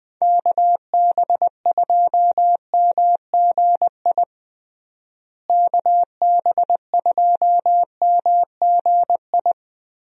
Morse code, heard as a keyed tone, about 20 wpm, 700 Hz. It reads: KB2MGI KB2MGI
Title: USA Call Signs - 6 Characters - Repeated 2x - 20wpm